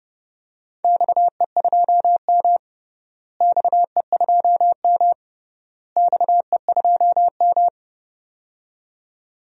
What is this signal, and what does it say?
Morse code, heard as a keyed tone, about 30 wpm, 700 Hz. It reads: XE2M XE2M XE2M
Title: DXCC-100 Call Signs - 4 Characters - Repeated 3x - 30wpm